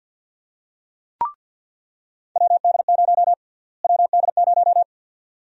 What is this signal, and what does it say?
Morse code, heard as a keyed tone, about 46 wpm, 700 Hz. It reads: WD0 WD0
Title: USA Call Sign Prefixes - Repeated 2x - 50wpm